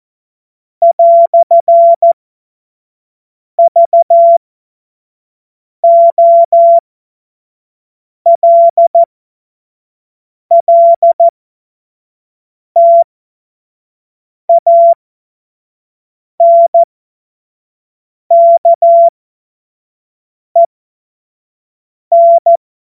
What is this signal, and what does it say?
Morse code, heard as a keyed tone, about 14 wpm, spaced as for 5 wpm, 680 Hz. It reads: "VOLLTANKEN
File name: Morsekurs / Witze / Witz 07. März 2023 25 F70